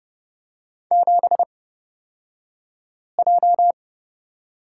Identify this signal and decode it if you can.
Morse code, heard as a keyed tone, about 30 wpm, spaced as for 5 wpm, 700 Hz. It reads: 7J
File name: Lesson 37d - Letter x - Two Characters 30wpm